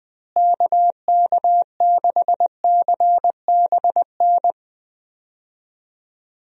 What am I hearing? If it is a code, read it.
Morse code, heard as a keyed tone, about 20 wpm, 700 Hz. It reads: KK6CBN
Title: USA Call Signs - 6 Characters - Repeated 2x - 20wpm